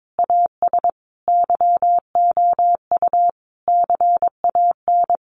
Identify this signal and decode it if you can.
Morse code, heard as a keyed tone, about 22 wpm, 700 Hz. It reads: AS YOU CAN
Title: Sentences from Top 1400 Words 22wpm